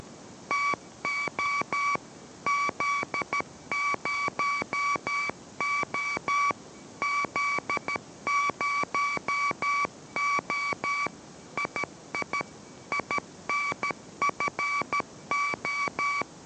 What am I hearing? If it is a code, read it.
Morse code, heard as a keyed tone, about 13 wpm, 2300 Hz, with noise. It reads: TO Z0O Z0O II INFO